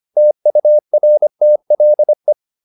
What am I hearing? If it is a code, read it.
Morse code, heard as a keyed tone, about 25 wpm, 600 Hz. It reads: TURTLE
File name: the old man and the sea part 1 25wpm